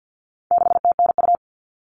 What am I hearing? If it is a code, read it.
Morse code, heard as a keyed tone, about 50 wpm, 700 Hz. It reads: 6TNU
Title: All Cut Numbers - 4-Digit 50wpm